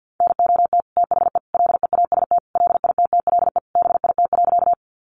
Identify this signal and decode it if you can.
Morse code, heard as a keyed tone, about 50 wpm, 700 Hz. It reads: NOT THE LEAST LITTLE BIT.
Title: Sentences from Top 900 Words 50wpm